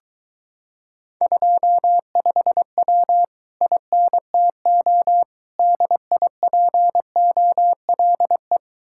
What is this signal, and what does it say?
Morse code, heard as a keyed tone, about 23 wpm, 700 Hz. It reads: 25W INTO DIPOLE